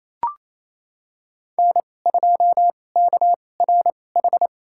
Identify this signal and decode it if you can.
Morse code, heard as a keyed tone, about 28 wpm, spaced as for 20 wpm, 700 Hz. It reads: N2KRH
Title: USA Call Signs - 5 Characters - 2x Character Spacing - 28wpm